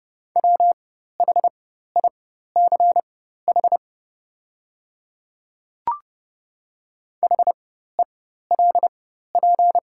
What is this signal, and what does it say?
Morse code, heard as a keyed tone, about 30 wpm, spaced as for 14 wpm, 690 Hz. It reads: WHICH HELP
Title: 4x Character Spacing - Top 300 Words - 30wpm